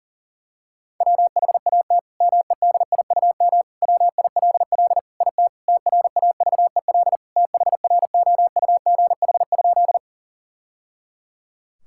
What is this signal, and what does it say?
Morse code, heard as a keyed tone, about 37 wpm, 700 Hz. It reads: WHAT MEDIUM WILL IT TRAVEL THROUGH?
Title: Sentences from Top 1000 Words - Review Part 6 - 40wpm